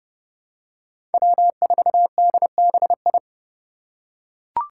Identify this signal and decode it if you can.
Morse code, heard as a keyed tone, about 30 wpm, 700 Hz. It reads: W4DBI
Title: Five Challenging Letter-Number Combinations 30wpm